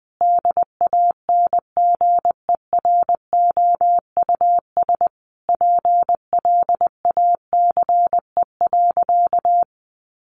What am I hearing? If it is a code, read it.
Morse code, heard as a keyed tone, about 20 wpm, 700 Hz. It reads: DANGEROUS PLACE.
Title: Sentences from Top 1200 Words 20wpm